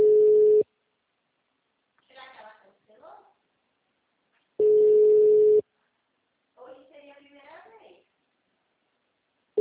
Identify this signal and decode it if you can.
Ringback tone (Europe, 425 Hz), with speech.